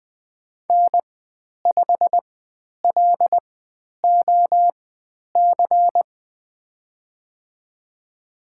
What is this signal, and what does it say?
Morse code, heard as a keyed tone, about 20 wpm, spaced as for 10 wpm, 700 Hz. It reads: N5LOC
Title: 109 Lesson 18g - Letter m - Call Signs 20wpm